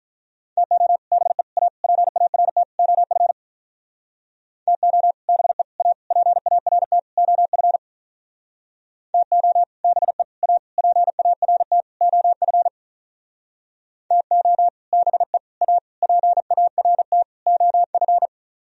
Morse code, heard as a keyed tone, about 45 wpm, 700 Hz. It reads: TO BE A PART OF TO BE A PART OF TO BE A PART OF TO BE A PART OF